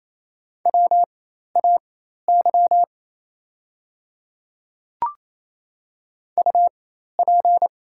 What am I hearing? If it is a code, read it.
Morse code, heard as a keyed tone, about 27 wpm, spaced as for 13 wpm, 700 Hz. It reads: WAY UP
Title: 4x Character Spacing - Top 100 Words - 28wpm